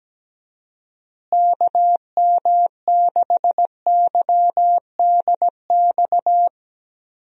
Morse code, heard as a keyed tone, about 17 wpm, 700 Hz. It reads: KM6YDX
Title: USA Call Signs - 6 Characters - Repeated 3x - 17wpm